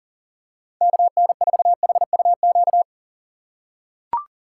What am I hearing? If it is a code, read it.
Morse code, heard as a keyed tone, about 38 wpm, 700 Hz. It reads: KN4HUQ
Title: USA Call Signs - 2x Character Spacing - 40wpm